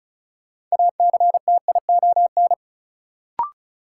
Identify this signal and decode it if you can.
Morse code, heard as a keyed tone, about 34 wpm, 700 Hz. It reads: ACTION
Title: Sentences from Top 700 Words - New Word Drill - 35wpm